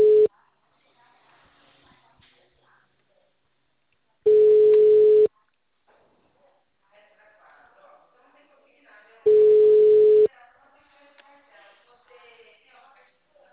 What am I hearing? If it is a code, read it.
Ringback tone (Europe, 425 Hz), with speech.
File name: out- -2004- - - .90561